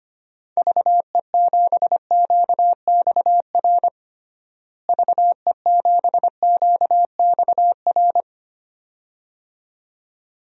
Morse code, heard as a keyed tone, about 25 wpm, 700 Hz. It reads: VE7QXR VE7QXR
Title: DXCC-100 Call Signs - 6 Characters - Repeated 2x - 25wpm